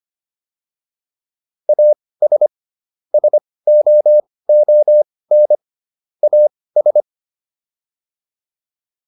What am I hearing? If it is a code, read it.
Morse code, heard as a keyed tone, about 25 wpm, spaced as for 18 wpm, 600 Hz. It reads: AS SOON AS
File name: WD204 18